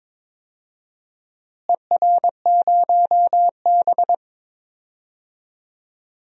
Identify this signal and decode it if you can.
Morse code, heard as a keyed tone, about 21 wpm, 700 Hz. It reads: ER0B
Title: DXCC-100 Call Signs - 4 Characters - Repeated 3x - 22wpm